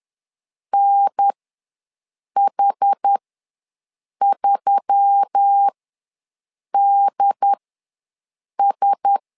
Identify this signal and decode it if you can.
Morse code, heard as a keyed tone, about 10 wpm, spaced as for 6 wpm, 800 Hz. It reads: NH3DS
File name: LEZIONI / 4 Q1S3 ws 10wpm 2437